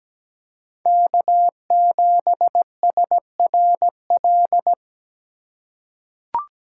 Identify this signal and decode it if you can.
Morse code, heard as a keyed tone, about 17 wpm, 700 Hz. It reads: K7SRL
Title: USA Call Signs - 5 Characters - 4x Character Spacing - 17wpm